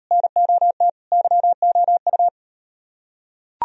Morse code, heard as a keyed tone, about 37 wpm, 700 Hz. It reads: NOT YOU